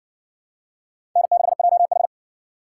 Morse code, heard as a keyed tone, about 60 wpm, 700 Hz. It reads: N6JH